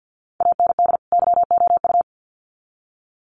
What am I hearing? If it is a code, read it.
Morse code, heard as a keyed tone, about 50 wpm, 700 Hz. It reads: AND YOU
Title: Sets of 2 Words 50wpm